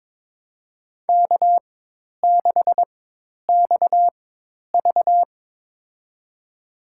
Morse code, heard as a keyed tone, about 22 wpm, spaced as for 10 wpm, 700 Hz. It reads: K6XV
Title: USA Call Signs - 4 Characters - 4x Character Spacing - 22wpm